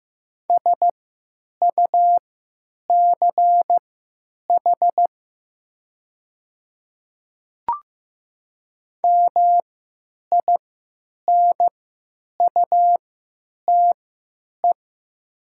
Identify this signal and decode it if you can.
Morse code, heard as a keyed tone, about 15 wpm, spaced as for 9 wpm, 700 Hz. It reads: SUCH MINUTE
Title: 3x Character Spacing - Top 500 Words - 15wpm